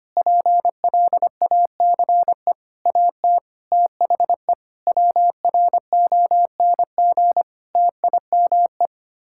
Morse code, heard as a keyed tone, about 25 wpm, 700 Hz. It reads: PLACE AT THE WRONG TIME
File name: Sets of 9 Words - 25wpm